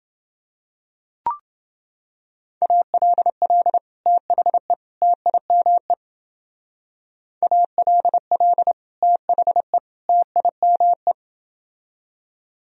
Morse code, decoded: ALL THE TIME ALL THE TIME

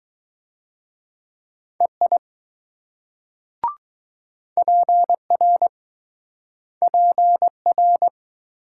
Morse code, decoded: EI PR PR

